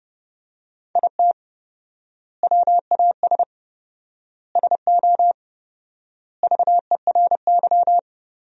Morse code, heard as a keyed tone, about 32 wpm, 700 Hz. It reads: IT WAS SO VERY
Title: 4x Word-Spacing - Sentences from Top 500 Words - 30wpm